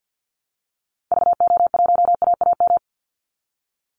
Morse code, heard as a keyed tone, about 50 wpm, 700 Hz. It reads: VO1AAM
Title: Call Signs Repeated 2x - 50wpm